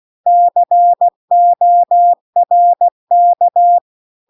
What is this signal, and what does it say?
Morse code, heard as a keyed tone, about 16 wpm, 700 Hz. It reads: CORK